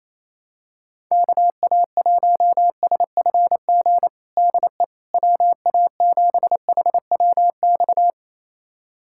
Morse code, heard as a keyed tone, about 28 wpm, 700 Hz. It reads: KA1SFG DE WA7HWX
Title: USA Callsign DE Callsign 28wpm